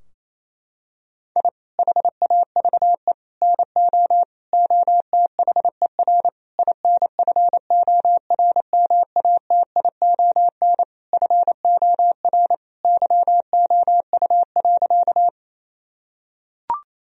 Morse code, heard as a keyed tone, about 28 wpm, 700 Hz. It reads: I HAVE NO OTHER INFORMATION FOR YOU.
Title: Sentences from Top 1000 Words - Review Part 3 - 28wpm